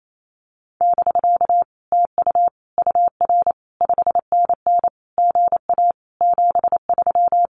Morse code, heard as a keyed tone, about 28 wpm, 700 Hz. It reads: <BK> TU UR 5NN GA 73
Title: Common QSO Phrases Encore 28wpm